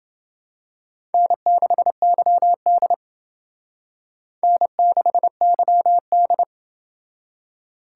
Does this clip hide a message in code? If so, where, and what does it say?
Yes: Morse code from 1.1 to 6.4 s, heard as a keyed tone, about 29 wpm, 700 Hz. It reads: N6YD N6YD